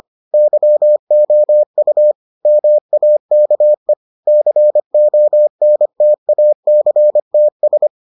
Morse code, heard as a keyed tone, about 25 wpm, 600 Hz. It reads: YOU MAKE CONTACTS